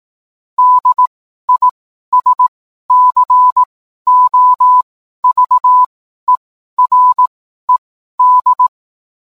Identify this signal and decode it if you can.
Morse code, heard as a keyed tone, about 18 wpm, spaced as for 12 wpm, 1000 Hz. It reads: DISCOVERED